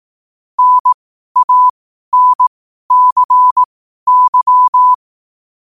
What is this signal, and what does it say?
Morse code, heard as a keyed tone, about 18 wpm, spaced as for 12 wpm, 1000 Hz. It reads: NANCY